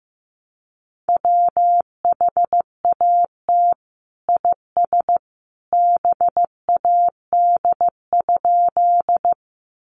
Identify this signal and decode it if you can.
Morse code, heard as a keyed tone, about 15 wpm, 700 Hz. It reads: WHAT IS BAD?
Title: Sentences from Top 700 Words 15wpm